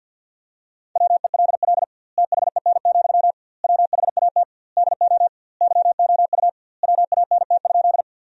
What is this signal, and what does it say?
Morse code, heard as a keyed tone, about 52 wpm, 700 Hz. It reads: WELL THEN, WHAT DO YOU WANT?
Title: 2x Word-Spacing - Sentences from Top 200 Words - 50wpm